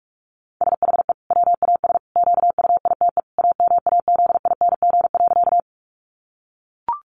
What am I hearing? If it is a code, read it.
Morse code, heard as a keyed tone, about 44 wpm, 700 Hz. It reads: SHE WAS QUITE AMAZING.
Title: Sentences from Top 1500 Words 45wpm